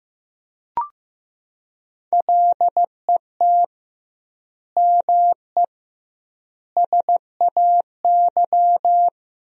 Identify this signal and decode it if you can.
Morse code, heard as a keyed tone, about 15 wpm, 700 Hz. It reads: LET ME SAY
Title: 2x Word-Spacing - Sentences from Top 300 Words - 15wpm